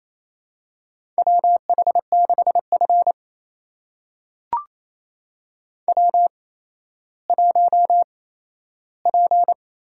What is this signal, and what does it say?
Morse code, heard as a keyed tone, about 27 wpm, spaced as for 13 wpm, 700 Hz. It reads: WH6F W1P
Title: USA Call Signs - 4 Characters - 8x Character Spacing - 28wpm